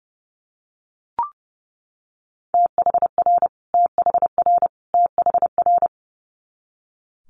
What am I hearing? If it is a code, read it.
Morse code, heard as a keyed tone, about 30 wpm, 700 Hz. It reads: THR THR THR
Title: Multi-letter Phonemes Repeated 3x 30wpm